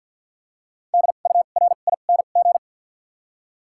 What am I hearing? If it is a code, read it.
Morse code, heard as a keyed tone, about 53 wpm, spaced as for 37 wpm, 700 Hz. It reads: DURING
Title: 2x Character Spacing - Top 500 Words - 50wpm